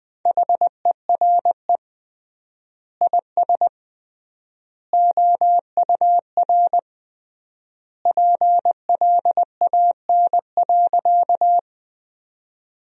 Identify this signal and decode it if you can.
Morse code, heard as a keyed tone, about 20 wpm, 700 Hz. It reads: HERE IS OUR PLAN.
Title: 3x Word-Spacing - Sentences from Top 500 Words - 20wpm